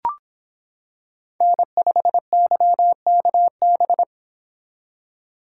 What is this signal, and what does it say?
Morse code, heard as a keyed tone, about 26 wpm, 700 Hz. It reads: N5YKB